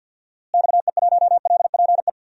Morse code, heard as a keyed tone, about 53 wpm, 700 Hz. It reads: XE1LPE
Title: DXCC-100 Call Signs - 6 Characters - Repeated 3x - 50wpm